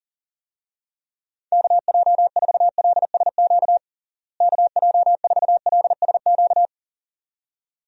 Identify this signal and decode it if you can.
Morse code, heard as a keyed tone, about 40 wpm, 700 Hz. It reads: KJ4LSQ KJ4LSQ
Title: USA Call Signs - 6 Characters - Repeated 2x - 40wpm